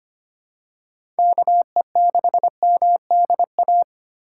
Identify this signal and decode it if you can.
Morse code, heard as a keyed tone, about 25 wpm, 700 Hz. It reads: KE6MDA
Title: USA Call Signs - 6 Characters - 4x Character Spacing - 25wpm